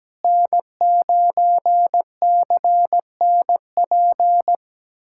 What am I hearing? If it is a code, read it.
Morse code, heard as a keyed tone, about 17 wpm, 700 Hz. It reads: N9CNP
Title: USA Callsign DE Callsign - 5 Characters - 17wpm